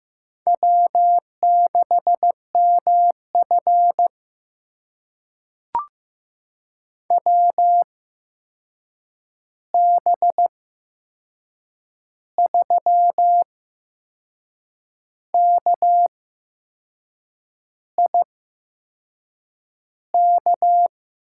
Morse code, heard as a keyed tone, about 15 wpm, spaced as for 6 wpm, 700 Hz. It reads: W6MF WB3KIK
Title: USA Call Signs - 8x Character Spacing - 15wpm